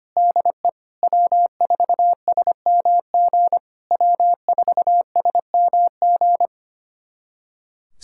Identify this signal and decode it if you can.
Morse code, heard as a keyed tone, about 25 wpm, 700 Hz. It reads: DE W4SMG W4SMG
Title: SOTA QSO Practice 25wpm